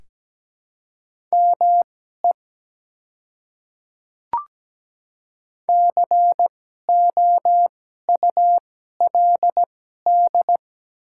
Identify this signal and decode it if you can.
Morse code, heard as a keyed tone, about 17 wpm, spaced as for 12 wpm, 700 Hz. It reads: ME COULD
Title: 2x Character Spacing - Top 200 Words - 17wpm